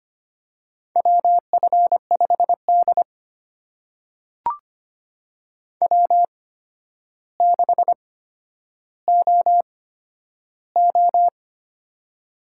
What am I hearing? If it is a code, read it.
Morse code, heard as a keyed tone, about 25 wpm, spaced as for 11 wpm, 700 Hz. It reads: WF5D W6OO
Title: USA Call Signs - 4 Characters - 8x Character Spacing - 25wpm